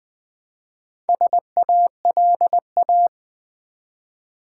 Morse code, heard as a keyed tone, about 20 wpm, 700 Hz. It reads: SALA
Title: Build A Word - 1 Letter At A Time - 5-Letter Words - 20wpm